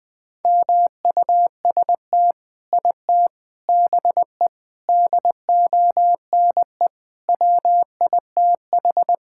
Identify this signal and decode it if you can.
Morse code, heard as a keyed tone, about 21 wpm, 700 Hz. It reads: MUST IT BE DONE WITH
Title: Sentences from Top 1000 Words - Review Part 5 - 20wpm